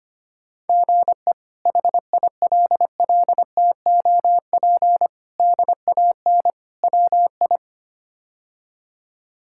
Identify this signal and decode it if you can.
Morse code, heard as a keyed tone, about 25 wpm, 700 Hz. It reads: GE HILLTOP DAN WI
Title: K1USN SST QSO Practice - 25wpm